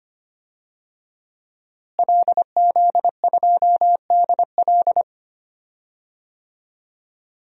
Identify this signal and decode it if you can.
Morse code, heard as a keyed tone, about 25 wpm, 700 Hz. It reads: LZ2DL
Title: DXCC-100 Call Signs - 5 Characters - Repeated 3x - 25wpm